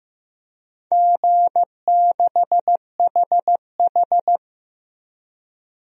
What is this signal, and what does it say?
Morse code, heard as a keyed tone, about 15 wpm, 700 Hz. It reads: G6HH
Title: DXCC-100 Call Signs - 4 Characters - 15wpm